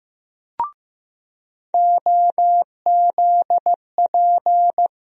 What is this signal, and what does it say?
Morse code, heard as a keyed tone, about 15 wpm, 700 Hz. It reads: OZP
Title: USA Call Sign Postfixes - 3 Letters 15wpm